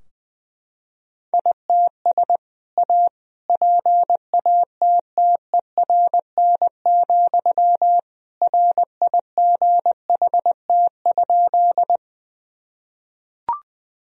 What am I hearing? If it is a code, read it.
Morse code, heard as a keyed tone, about 19 wpm, 700 Hz. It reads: ITS A PATTERN, RIGHT?
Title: Sentences from Top 1000 Words - Review Part 2 - 20wpm